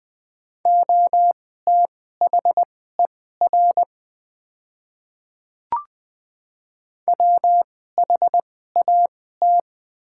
Morse code, heard as a keyed tone, about 20 wpm, spaced as for 15 wpm, 700 Hz. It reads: OTHER WHAT
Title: 2x Character Spacing - Top 100 Words - 20wpm